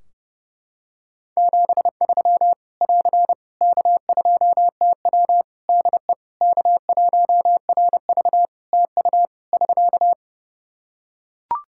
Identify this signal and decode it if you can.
Morse code, heard as a keyed tone, about 30 wpm, 700 Hz. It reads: 73 <AR> K2TW DE K1RV TU <SK>
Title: K1USN SST QSO Practice 30wpm